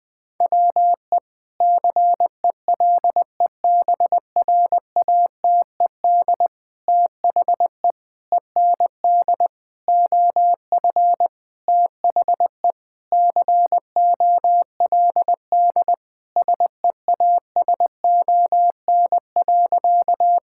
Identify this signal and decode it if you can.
Morse code, heard as a keyed tone, about 20 wpm, 700 Hz. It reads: WE CELEBRATED THE END OF THE COLD SEASON.
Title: Sentences with 5-Word Phrases - 20wpm